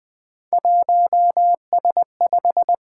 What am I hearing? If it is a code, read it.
Morse code, heard as a keyed tone, about 20 wpm, 700 Hz. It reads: 1S5
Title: Three Letter-Number Combinations - 20wpm